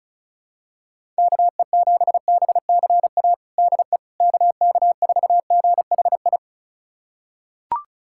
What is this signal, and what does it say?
Morse code, heard as a keyed tone, about 36 wpm, 700 Hz. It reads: KE7BCA DE KK4GHI
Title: USA Callsign DE Callsign 35wpm